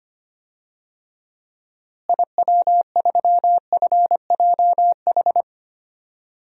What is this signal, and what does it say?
Morse code, heard as a keyed tone, about 25 wpm, 700 Hz. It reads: IW3FJH
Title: DXCC-100 Call Signs - 6 Characters - Repeated 3x - 25wpm